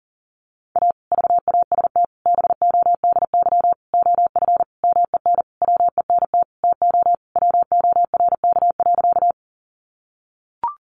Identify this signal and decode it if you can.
Morse code, heard as a keyed tone, about 40 wpm, 700 Hz. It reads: A VAST BODY OF MEN WENT TO WORK.
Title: Sentences from Top 1300 Words 40wpm